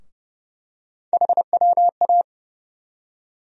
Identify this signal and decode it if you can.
Morse code, heard as a keyed tone, about 30 wpm, 700 Hz. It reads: HWA
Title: USA Call Sign Postfixes - 3 Letters 30wpm